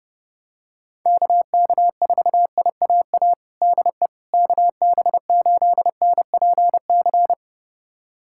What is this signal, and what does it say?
Morse code, heard as a keyed tone, about 30 wpm, 690 Hz. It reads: KK4IAA DE KB8NPC
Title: USA Callsign DE Callsign - 6 Characters - 30wpm